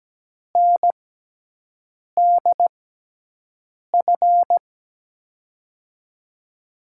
Morse code, heard as a keyed tone, about 17 wpm, spaced as for 6 wpm, 700 Hz. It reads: NDF
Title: USA Call Signs - 6 Characters - 6x Character Spacing - 17wpm